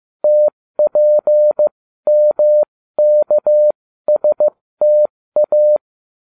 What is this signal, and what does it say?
Morse code, heard as a keyed tone, about 15 wpm, spaced as for 13 wpm, 600 Hz. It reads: TPMKSTA